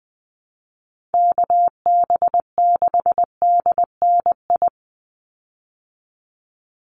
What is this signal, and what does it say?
Morse code, heard as a keyed tone, about 20 wpm, 700 Hz. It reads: KB6DNI
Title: USA Call Signs 20wpm